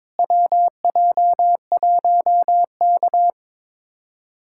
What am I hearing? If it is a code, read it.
Morse code, heard as a keyed tone, about 21 wpm, 700 Hz. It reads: WJ1K
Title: USA Callsign DE Callsign - 4 Characters - 22wpm